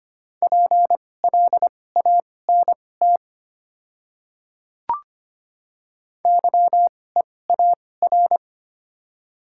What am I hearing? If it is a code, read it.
Morse code, heard as a keyed tone, about 25 wpm, spaced as for 18 wpm, 700 Hz. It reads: PLANT YEAR
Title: 2x Character Spacing - Top 300 Words - 25wpm